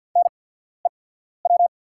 Morse code, heard as a keyed tone, about 48 wpm, spaced as for 14 wpm, 700 Hz. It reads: NEW
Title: 8x Character Spacing - Top 300 Words - 50wpm